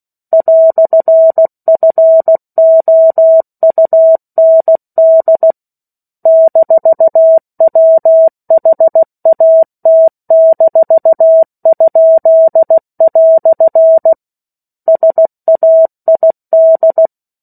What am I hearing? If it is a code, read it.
Morse code, heard as a keyed tone, about 16 wpm, 650 Hz. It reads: "FOUND -WHAT-?" SAID
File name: 16 WPM